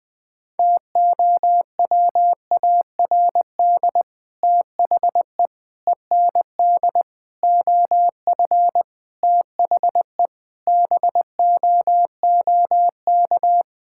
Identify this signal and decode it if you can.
Morse code, heard as a keyed tone, about 20 wpm, 700 Hz. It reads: TOWARD THE END OF THE BOOK
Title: Sentences with 5-Word Phrases - 20wpm